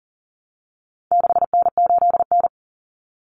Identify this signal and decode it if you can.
Morse code, heard as a keyed tone, about 40 wpm, 700 Hz. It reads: 6N8N
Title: Common Cut Numbers - 4-Digit 40wpm